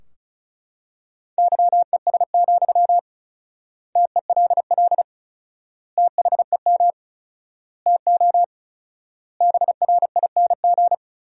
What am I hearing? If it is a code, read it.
Morse code, heard as a keyed tone, about 36 wpm, 700 Hz. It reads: YES, TELL THEM TO BRING
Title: 4x Word-Spacing - Sentences from Top 500 Words - 35wpm